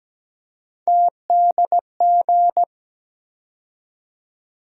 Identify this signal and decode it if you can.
Morse code, heard as a keyed tone, about 17 wpm, 700 Hz. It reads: TDG
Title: USA Call Sign Postfixes - 3 Letters 17wpm